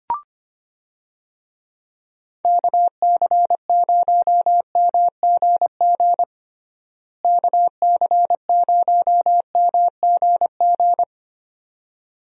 Morse code, heard as a keyed tone, about 25 wpm, 700 Hz. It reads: KC0MGG KC0MGG
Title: USA Call Signs - 6 Characters - Repeated 2x - 25wpm